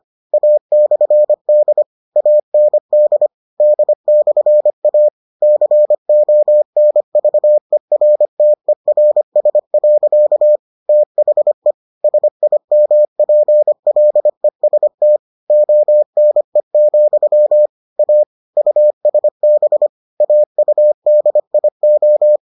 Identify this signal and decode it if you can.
Morse code, heard as a keyed tone, about 25 wpm, 600 Hz. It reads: A/D AND D/A CONVERTERS. THE SIMPLEST ONE, A USB AUDIO